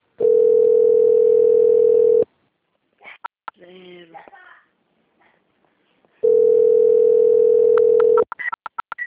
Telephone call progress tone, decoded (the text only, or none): ringback tone